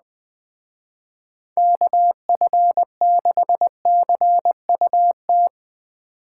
Morse code, heard as a keyed tone, about 21 wpm, 700 Hz. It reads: KF6CUT